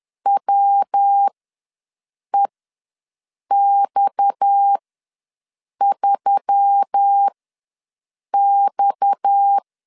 Morse code, heard as a keyed tone, about 10 wpm, spaced as for 6 wpm, 800 Hz. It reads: WEX3X
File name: LEZIONI / 6 PKX0 / 6 PKX ws 10wpm 3706